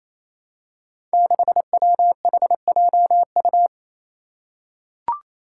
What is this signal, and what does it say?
Morse code, heard as a keyed tone, about 27 wpm, 700 Hz. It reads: 6WHJU
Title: Five Challenging Letter-Number Combinations 28wpm